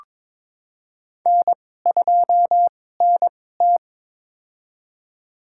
Morse code, heard as a keyed tone, about 22 wpm, spaced as for 16 wpm, 700 Hz. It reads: N2NT
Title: USA Call Signs - 4 Characters - 2x Character Spacing - 22wpm